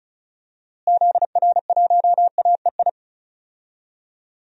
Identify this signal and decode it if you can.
Morse code, heard as a keyed tone, about 34 wpm, 700 Hz. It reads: ZR1AEI
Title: DXCC-100 Call Signs - 6 Characters - Repeated 3x - 35wpm